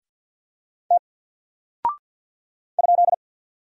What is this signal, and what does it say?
Morse code, heard as a keyed tone, about 48 wpm, 700 Hz. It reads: T ?
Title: Single Character (Letter:Number:Punctuation:Prosign) Alt 50wpm